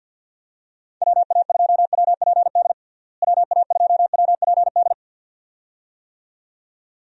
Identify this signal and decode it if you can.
Morse code, heard as a keyed tone, about 50 wpm, 700 Hz. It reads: WA2WPD WA2WPD
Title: USA Call Signs - 6 Characters - Repeated 2x - 50wpm